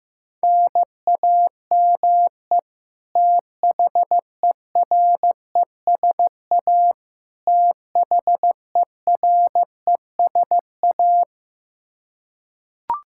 Morse code, heard as a keyed tone, about 15 wpm, 700 Hz. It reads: NAME THERESA THERESA
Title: Practice QSO - Exchange 1 - Beginning - 15wpm